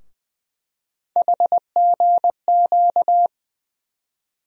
Morse code, heard as a keyed tone, about 21 wpm, 700 Hz. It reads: HGQ
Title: USA Call Sign Postfixes - 3 Letters 20wpm